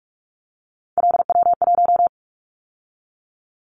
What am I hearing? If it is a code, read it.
Morse code, heard as a keyed tone, about 44 wpm, 700 Hz. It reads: LW1
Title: Callsign Prefixes for Top 50 DXCC - 45wpm